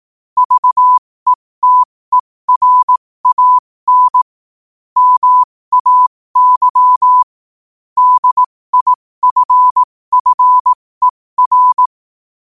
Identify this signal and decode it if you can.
Morse code, heard as a keyed tone, about 18 wpm, spaced as for 15 wpm, 1000 Hz. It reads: VETERAN MAY DIFFER
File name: qotdfiles / 15wpm